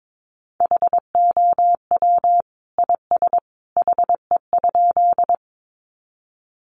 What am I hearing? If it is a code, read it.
Morse code, heard as a keyed tone, about 22 wpm, 700 Hz. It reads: HOW IS HE?
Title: Sentences from Top 100 Words 22wpm